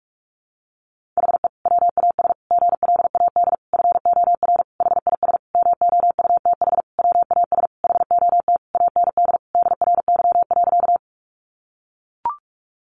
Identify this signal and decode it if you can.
Morse code, heard as a keyed tone, about 44 wpm, 700 Hz. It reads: HE WAS GLAD FOR HIS MOUTH WAS HOT AND DRY.
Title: Sentences from Top 1500 Words 45wpm